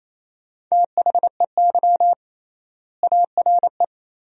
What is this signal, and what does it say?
Morse code, heard as a keyed tone, about 28 wpm, 700 Hz. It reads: THEY ARE
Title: 3x Word-Spacing - Sentences from Top 300 Words - 28wpm